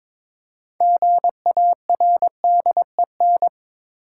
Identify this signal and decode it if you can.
Morse code, heard as a keyed tone, about 21 wpm, 700 Hz. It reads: GARDEN
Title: Six Letter Words - 22wpm